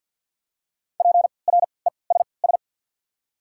Morse code, heard as a keyed tone, about 54 wpm, 690 Hz. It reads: P R E S S
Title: 3x Character Spacing - Top 300 Words - 50wpm